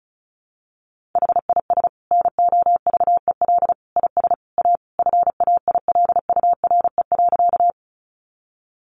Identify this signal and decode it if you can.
Morse code, heard as a keyed tone, about 34 wpm, 700 Hz. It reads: HIS NOVEL IS A FAILURE.
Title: Sentences from Top 1300 Words 35wpm